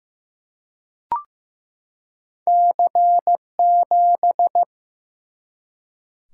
Morse code, heard as a keyed tone, about 15 wpm, 700 Hz. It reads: C7
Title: Single-Letter Single-Number 15wpm